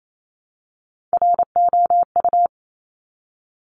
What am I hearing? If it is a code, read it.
Morse code, heard as a keyed tone, about 28 wpm, 700 Hz. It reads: ROU
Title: Top Three Letter Combinations in Words 28wpm